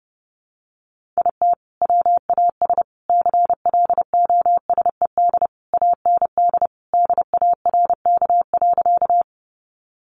Morse code, heard as a keyed tone, about 30 wpm, 700 Hz. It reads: IT WAS CLOSED AND DARK.
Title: Sentences from Top 1400 Words 30wpm